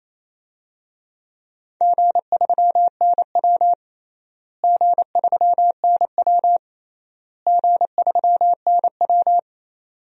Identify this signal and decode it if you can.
Morse code, heard as a keyed tone, about 28 wpm, 700 Hz. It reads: G3NW G3NW G3NW
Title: DXCC-100 Call Signs - 4 Characters - Repeated 3x - 28wpm